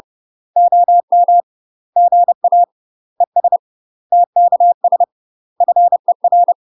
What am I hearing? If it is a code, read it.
Morse code, heard as a keyed tone, about 30 wpm, 700 Hz. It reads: OM GA ES TKS FER
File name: UK MASTER QSOs TRAINING / 30WPM / UK MASTER QSOs 0014